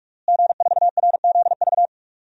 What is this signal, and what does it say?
Morse code, heard as a keyed tone, about 46 wpm, 700 Hz. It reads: G4RZV
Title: DXCC-100 Call Signs - 5 Characters - Repeated 3x - 45wpm